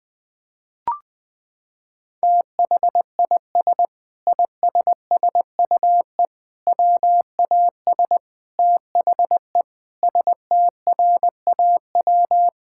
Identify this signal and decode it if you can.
Morse code, heard as a keyed tone, about 21 wpm, 700 Hz. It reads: THIS ISSUE WAS THE STRAW
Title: Sentences with 7-Word Phrases - 20wpm